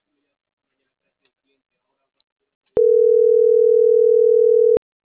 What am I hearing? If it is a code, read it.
Ringback tone (North America), 440 + 480 Hz.